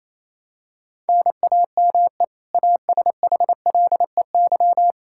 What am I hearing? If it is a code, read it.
Morse code, heard as a keyed tone, about 28 wpm, 700 Hz. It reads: NAME ASHLEY
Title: Practice QSO - Exchange 1 - Beginning - 28wpm